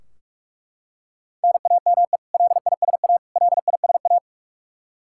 Morse code, heard as a keyed tone, about 42 wpm, 700 Hz. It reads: NAME LISA LISA
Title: Practice QSO - Exchange 1 - Beginning - 45wpm